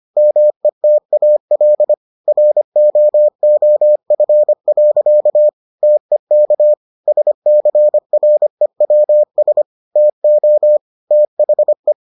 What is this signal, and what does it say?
Morse code, heard as a keyed tone, about 25 wpm, 600 Hz. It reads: METAL ROOF. TEK SCREWS TO THE